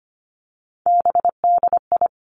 Morse code, heard as a keyed tone, about 25 wpm, 700 Hz. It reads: BDI
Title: Three Challenging Letter-Number Combinations 25wpm